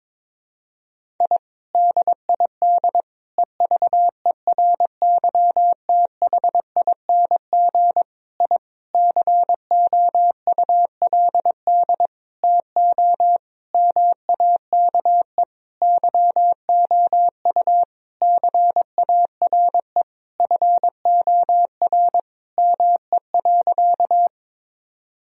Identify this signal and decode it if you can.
Morse code, heard as a keyed tone, about 22 wpm, 700 Hz. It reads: I DID EVERYTHING I COULD TO MAKE YOU CARE FOR ME.
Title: Sentences from Top 1000 Words - Review Part 1 - 22wpm